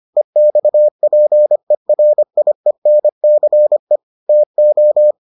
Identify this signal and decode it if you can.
Morse code, heard as a keyed tone, about 25 wpm, 600 Hz. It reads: EXPERIENCE TO